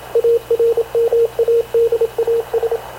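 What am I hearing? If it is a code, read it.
Morse code, heard as a keyed tone, about 27 wpm, 460 Hz, with radio noise. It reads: ARMADAS
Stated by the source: Havana Radio CLA press (PX) broadcast, recorded by Radio Officer Paul Dumesnil aboard mv CNAV QUEST/CZDO.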